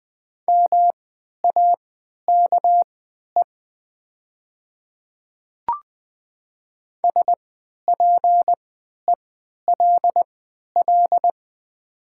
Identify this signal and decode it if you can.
Morse code, heard as a keyed tone, about 20 wpm, spaced as for 11 wpm, 700 Hz. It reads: MAKE SPELL
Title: 3x Character Spacing - Top 300 Words - 20wpm